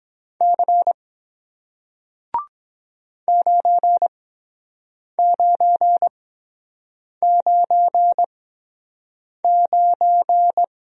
Morse code, heard as a keyed tone, about 23 wpm, spaced as for 7 wpm, 700 Hz. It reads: C 9999